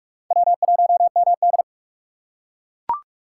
Morse code, heard as a keyed tone, about 41 wpm, 700 Hz. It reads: W1MD